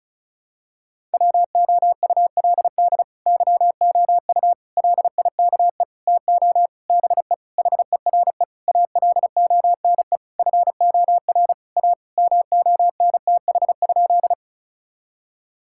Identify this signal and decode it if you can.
Morse code, heard as a keyed tone, about 34 wpm, 700 Hz. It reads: WOULD YOU LIKE TO BE HERE ALONE FOR A MONTH?
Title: Sentences from Top 1000 Words - Review Part 5 - 35wpm